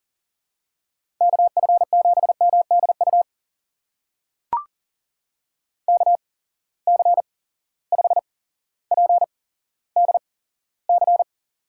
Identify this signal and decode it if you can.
Morse code, heard as a keyed tone, about 40 wpm, spaced as for 17 wpm, 700 Hz. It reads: KF7MDU KC5PDC
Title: USA Call Signs - 6 Characters - 8x Character Spacing - 40wpm